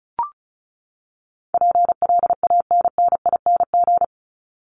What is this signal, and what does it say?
Morse code, heard as a keyed tone, about 34 wpm, 700 Hz. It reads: PLANNING
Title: ING Suffix - Top 100 Words with Base Words 35wpm